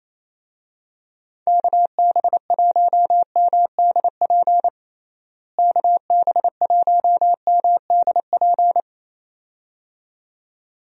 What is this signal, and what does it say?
Morse code, heard as a keyed tone, about 28 wpm, 700 Hz. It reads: KB1MDP KB1MDP
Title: USA Call Signs - 6 Characters - Repeated 2x - 28wpm